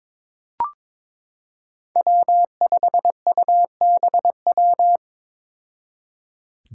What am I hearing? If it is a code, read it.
Morse code, heard as a keyed tone, about 22 wpm, 700 Hz. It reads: W5UBW
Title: Five Challenging Letter-Number Combinations 22wpm